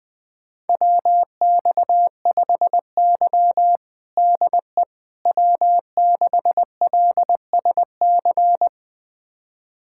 Morse code, heard as a keyed tone, about 19 wpm, 700 Hz. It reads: WX5Y DE W6LSC
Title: USA Callsign DE Callsign 20wpm